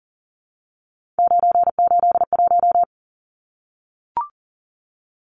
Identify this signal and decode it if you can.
Morse code, heard as a keyed tone, about 40 wpm, 700 Hz. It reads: 981
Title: Numbers - 3-Digit 40wpm